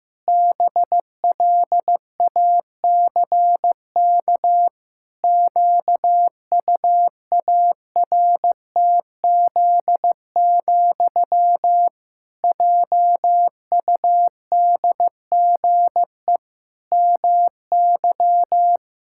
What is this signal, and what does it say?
Morse code, heard as a keyed tone, about 15 wpm, 700 Hz. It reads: BLACK QUARTZ, JUDGE MY
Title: Pangram Challenge v2 - 15wpm